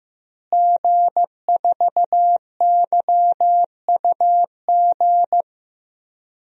Morse code, heard as a keyed tone, about 15 wpm, 700 Hz. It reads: G4YUG
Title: DXCC-100 Call Signs - 5 Characters - Repeated 3x - 15wpm